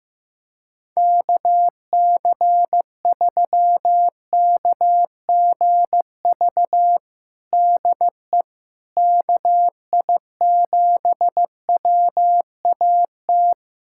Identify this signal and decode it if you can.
Morse code, heard as a keyed tone, about 16 wpm, 700 Hz. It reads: KC3KGV DE KI7WAT
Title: USA Callsign DE Callsign 15wpm